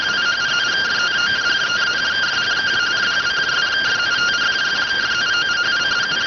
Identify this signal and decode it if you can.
Two-tone teleprinter signal, about 170 Hz shift.